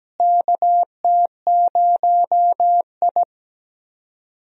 Morse code, heard as a keyed tone, about 17 wpm, 700 Hz. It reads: KT0I
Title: USA Call Signs - 4 Characters - Repeated 2x - 17wpm